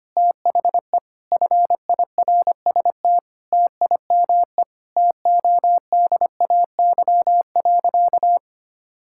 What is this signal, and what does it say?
Morse code, heard as a keyed tone, about 25 wpm, 700 Hz. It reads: THE FIRST TIME TODAY.
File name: practice / 25wpm / Sentences with 4-Word Phrases - 25wpm